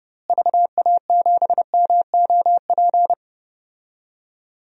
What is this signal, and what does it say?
Morse code, heard as a keyed tone, about 30 wpm, 700 Hz. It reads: VA7MOP
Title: DXCC-100 Call Signs - 6 Characters - Repeated 3x - 30wpm